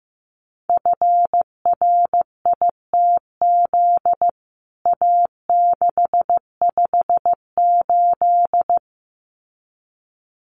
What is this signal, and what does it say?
Morse code, heard as a keyed tone, about 15 wpm, 700 Hz. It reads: FRITZ A658
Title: CWT Exchange 15wpm